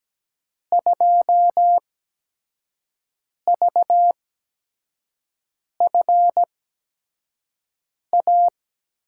Morse code, heard as a keyed tone, about 17 wpm, spaced as for 5 wpm, 700 Hz. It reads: 2VFA
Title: USA Call Signs - 6 Characters - 8x Character Spacing - 17wpm